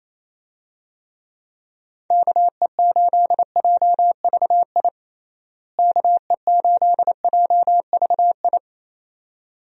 Morse code, heard as a keyed tone, about 28 wpm, 700 Hz. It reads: KE8JVI KE8JVI
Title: USA Call Signs - 6 Characters - Repeated 2x - 28wpm